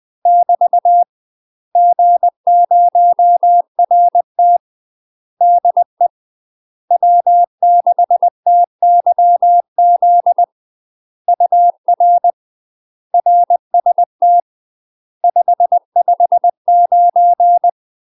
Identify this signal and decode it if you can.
Morse code, heard as a keyed tone, about 20 wpm, 700 Hz. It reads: <BT> G0RT DE W6TYZ UR RST 559
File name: UK MASTER QSOs TRAINING / 20WPM / UK MASTER QSOs 0066